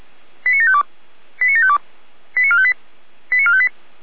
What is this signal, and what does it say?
Five-tone selective call (ZVEI), heard as quick runs of five tones, tones of about 70 ms. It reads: 89752 89752 89458 89458